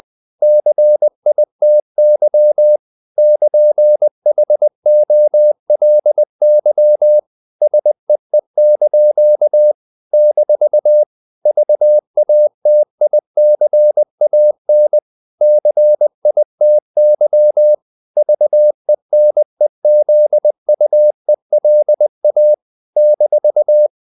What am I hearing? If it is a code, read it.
Morse code, heard as a keyed tone, about 20 wpm, 600 Hz. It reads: CITY <KN>HOLY SEE) - VATICAN CITY VENEZUELA -
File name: CW / Practice / NationalCapitals20